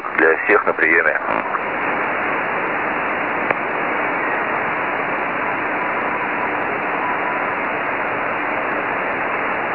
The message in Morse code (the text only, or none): none